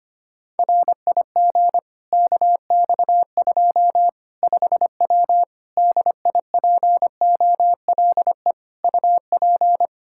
Morse code, heard as a keyed tone, about 25 wpm, 700 Hz. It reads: RIG KX2 5W DIPOLE UP
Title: Practice QSO - Exchange 2 - Info - 25wpm